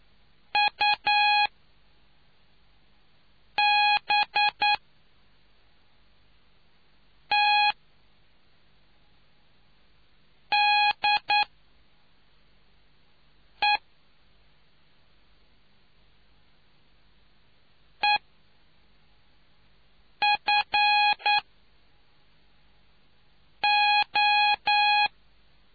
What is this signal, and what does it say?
Morse code, heard as a keyed tone, about 9 wpm, 830 Hz. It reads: U B T D E E F O